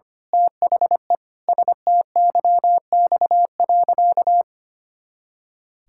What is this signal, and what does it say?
Morse code, heard as a keyed tone, about 25 wpm, 700 Hz. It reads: THE STYX.
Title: Pangram Challenge v2 - 25wpm